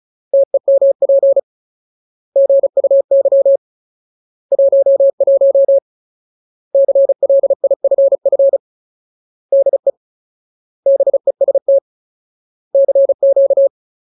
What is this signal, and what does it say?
Morse code, heard as a keyed tone, about 37 wpm, 550 Hz. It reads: TEMP GUY 11 CLIFF DE BEST CQ
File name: Ultimate Word List / 35WPM Ultimate